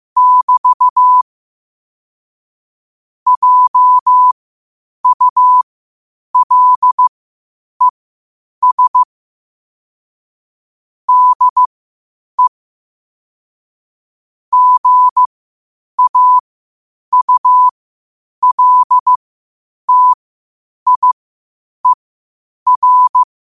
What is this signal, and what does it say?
Morse code, heard as a keyed tone, about 15 wpm, spaced as for 8 wpm, 1000 Hz. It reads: <BT> JULES DE GAULTIER
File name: qotdfiles / 8wpm